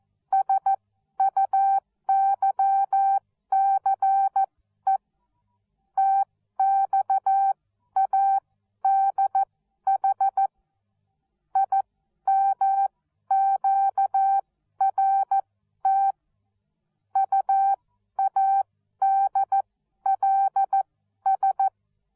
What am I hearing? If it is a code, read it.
Morse code, heard as a keyed tone, about 14 wpm, spaced as for 11 wpm, 780 Hz. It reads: SUYCE TXADH IMQRT UADLS
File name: Onlinelehrgang / morsen / DJ4UF Morsen13